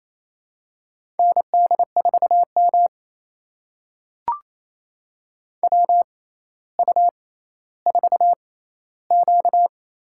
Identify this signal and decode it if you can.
Morse code, heard as a keyed tone, about 28 wpm, spaced as for 14 wpm, 700 Hz. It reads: ND4M WU4Q
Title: USA Call Signs - 4 Characters - 6x Character Spacing - 28wpm